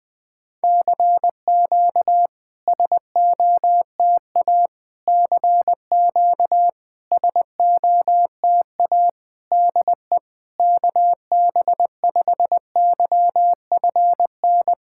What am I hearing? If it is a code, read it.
Morse code, heard as a keyed tone, about 20 wpm, 700 Hz. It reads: CQ SOTA CQ SOTA DE KB5YFN
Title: SOTA QSO Practice 20wpm